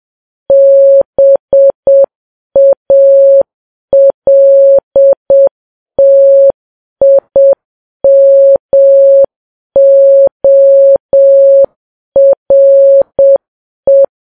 Morse code, heard as a keyed tone, about 7 wpm, 550 Hz. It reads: BALTIMORE
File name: Files Web tls / Morse on9fv / morse7kps